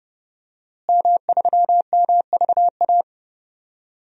Morse code, heard as a keyed tone, about 31 wpm, 700 Hz. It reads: M3MVA
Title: DXCC-100 Call Signs - 5 Characters - Repeated 2x - 30wpm